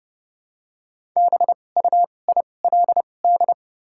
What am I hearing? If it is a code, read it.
Morse code, heard as a keyed tone, about 31 wpm, spaced as for 22 wpm, 700 Hz. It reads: BUILD
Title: 2x Character Spacing - Top 300 Words - 30wpm